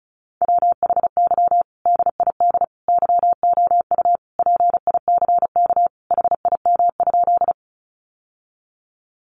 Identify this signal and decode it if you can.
Morse code, heard as a keyed tone, about 34 wpm, 700 Hz. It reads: WHY DID YOU PICK HIM?
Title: Sentences from Top 1000 Words 35wpm